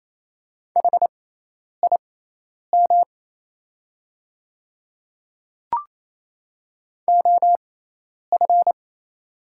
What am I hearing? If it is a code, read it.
Morse code, heard as a keyed tone, about 28 wpm, spaced as for 10 wpm, 690 Hz. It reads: HIM OF
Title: 6x Character Spacing - Top 100 Words - 28wpm